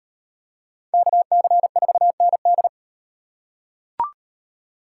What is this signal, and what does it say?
Morse code, heard as a keyed tone, about 35 wpm, 700 Hz. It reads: KC4ND